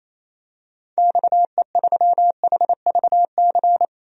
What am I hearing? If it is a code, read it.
Morse code, heard as a keyed tone, about 28 wpm, 700 Hz. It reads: XE3HVC
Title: DXCC-100 Call Signs - 6 Characters - 28wpm